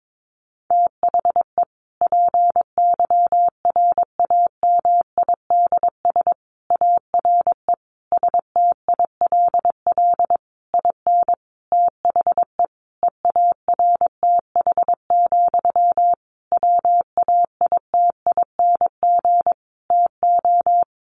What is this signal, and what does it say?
Morse code, heard as a keyed tone, about 22 wpm, 700 Hz. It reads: THE PYRAMIDS ARE STILL IN THE EARTH, WAITING TO
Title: A Sound Of Thunder 22wpm